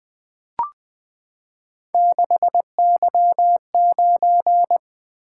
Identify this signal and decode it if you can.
Morse code, heard as a keyed tone, about 20 wpm, 700 Hz. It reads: 6Y9
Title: Three Letter-Number Combinations - 20wpm